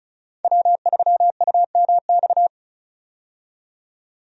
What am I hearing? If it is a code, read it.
Morse code, heard as a keyed tone, about 36 wpm, 700 Hz. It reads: W3UMX
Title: USA Call Signs - 5 Characters - 35wpm